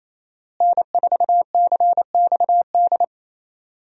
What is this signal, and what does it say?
Morse code, heard as a keyed tone, about 28 wpm, 700 Hz. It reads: N4CXD
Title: USA Call Signs - 5 Characters - Repeated 2x - 28wpm